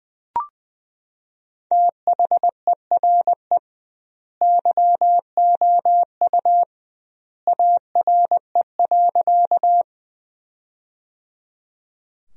Morse code, heard as a keyed tone, about 20 wpm, 700 Hz. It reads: THERE YOU ARE.
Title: 2x Word-Spacing - Sentences from Top 100 Words - 20wpm